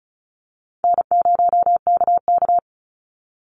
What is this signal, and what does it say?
Morse code, heard as a keyed tone, about 34 wpm, 700 Hz. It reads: N0KK
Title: Easy Call Signs Repeated 3x - 35wpm